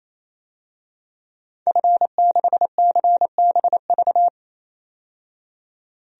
Morse code, heard as a keyed tone, about 28 wpm, 700 Hz. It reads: F6CBV
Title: DXCC-100 Call Signs - 5 Characters - 28wpm